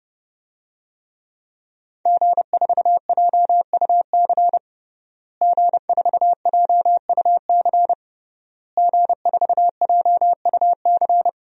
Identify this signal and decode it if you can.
Morse code, heard as a keyed tone, about 32 wpm, 700 Hz. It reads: G4JUC G4JUC G4JUC
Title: DXCC-100 Call Signs - 5 Characters - Repeated 3x - 30wpm